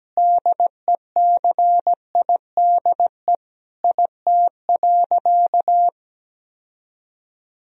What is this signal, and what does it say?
Morse code, heard as a keyed tone, about 17 wpm, 700 Hz. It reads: DECIDE IT.
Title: Sentences from Top 1000 Words - Review Part 3 - 17wpm